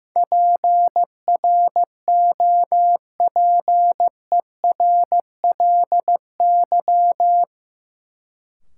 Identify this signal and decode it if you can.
Morse code, heard as a keyed tone, about 15 wpm, 700 Hz. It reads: PROPERLY
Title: Sentences from Top 1100 Words - New Word Drill Alt - 15wpm